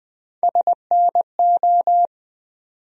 Morse code, heard as a keyed tone, about 20 wpm, 700 Hz. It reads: SNO
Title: USA Call Sign Postfixes - 3 Letters 20wpm